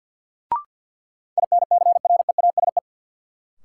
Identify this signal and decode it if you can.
Morse code, heard as a keyed tone, about 50 wpm, 700 Hz. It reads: INCREASE
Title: Sentences from Top 1000 Words - Review Part 4 - New Word Drill - 50wpm